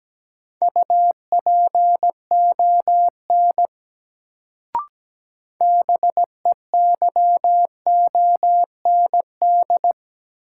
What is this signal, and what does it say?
Morse code, heard as a keyed tone, about 17 wpm, 700 Hz. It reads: UPON BEYOND
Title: Sentences from Top 1000 Words - Review Part 6 - New Word Drill - 17wpm